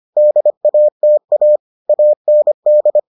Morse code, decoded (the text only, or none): DATA AND